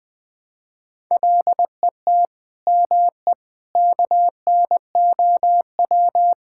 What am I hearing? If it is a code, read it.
Morse code, heard as a keyed tone, about 21 wpm, 700 Hz. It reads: LET ME KNOW
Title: Sentences with 8-Word Phrases - 20wpm